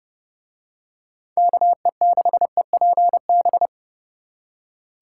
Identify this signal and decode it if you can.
Morse code, heard as a keyed tone, about 29 wpm, 700 Hz. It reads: KE6EPB